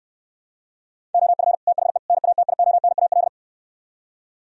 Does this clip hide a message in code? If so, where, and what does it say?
Yes: Morse code from 1.1 to 3.3 s, heard as a keyed tone, about 66 wpm, 700 Hz. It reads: OF THE NATIONAL